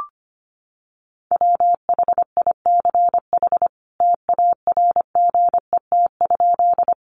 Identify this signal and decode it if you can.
Morse code, heard as a keyed tone, about 25 wpm, 700 Hz. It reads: WHICH TARGET?
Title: Sentences from Top 1200 Words 25wpm